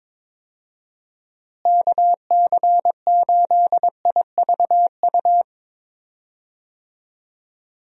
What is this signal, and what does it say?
Morse code, heard as a keyed tone, about 22 wpm, 700 Hz. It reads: KC8IVU
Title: USA Call Signs - 6 Characters - Repeated 3x - 22wpm